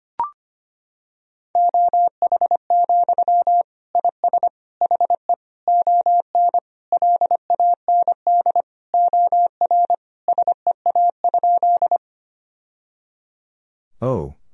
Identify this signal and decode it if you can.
Morse code, heard as a keyed tone, about 25 wpm, 700 Hz. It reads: OH, IS HE ON LAND OR SEA?
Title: Sentences from Top 1000 Words - Review Part 1 - 25wpm